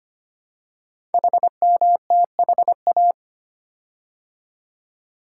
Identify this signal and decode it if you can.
Morse code, heard as a keyed tone, about 25 wpm, 700 Hz. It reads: HMTHA
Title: Code Groups - 5 Letters - 25wpm